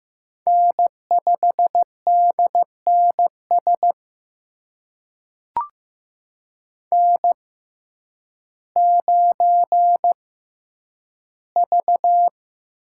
Morse code, heard as a keyed tone, about 15 wpm, spaced as for 9 wpm, 700 Hz. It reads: N5DNS N9V
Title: USA Call Signs - 5 Characters - 6x Character Spacing - 15wpm